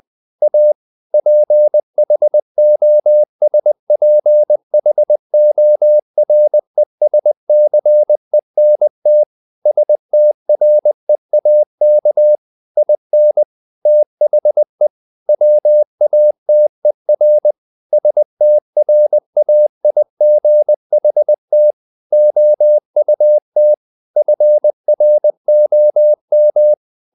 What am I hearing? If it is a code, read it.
Morse code, heard as a keyed tone, about 20 wpm, 600 Hz. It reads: A PHOSPHORESCENT STREAK IN THE WATER STRAIGHT OUT FROM
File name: the old man and the sea / the old man and the sea part 1 20wpm